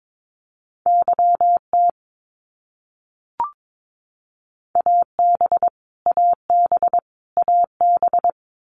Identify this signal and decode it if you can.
Morse code, heard as a keyed tone, about 22 wpm, 700 Hz. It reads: YT AB AB AB
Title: Canadian Provinces and Territories - Repeated 3x - 22wpm